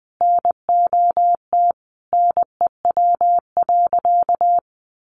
Morse code, heard as a keyed tone, about 19 wpm, 700 Hz. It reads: NOT NEW.
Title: Sentences from Top 1100 Words 20wpm